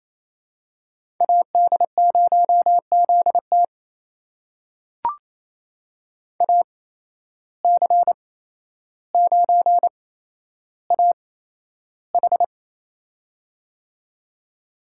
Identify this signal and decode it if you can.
Morse code, heard as a keyed tone, about 28 wpm, spaced as for 12 wpm, 700 Hz. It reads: AD0ZT AC9AH
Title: USA Call Signs - 5 Characters - 8x Character Spacing - 28wpm